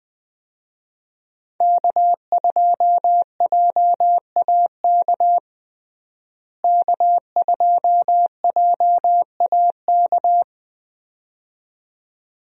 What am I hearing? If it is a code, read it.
Morse code, heard as a keyed tone, about 20 wpm, 700 Hz. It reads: K2JAK K2JAK
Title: USA Call Signs - 5 Characters - Repeated 2x - 20wpm